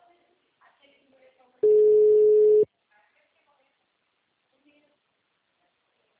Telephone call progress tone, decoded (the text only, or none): ringback tone